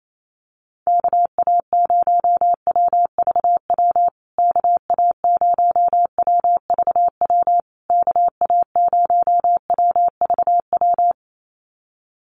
Morse code, heard as a keyed tone, about 28 wpm, 700 Hz. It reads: KA0WVW KA0WVW KA0WVW
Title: USA Call Signs Repeated 3x 28wpm